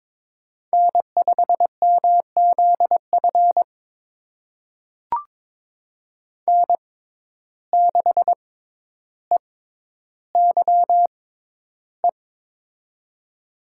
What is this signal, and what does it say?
Morse code, heard as a keyed tone, about 22 wpm, spaced as for 11 wpm, 700 Hz. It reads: N5MZF N6EYE
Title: USA Call Signs - 5 Characters - 6x Character Spacing - 22wpm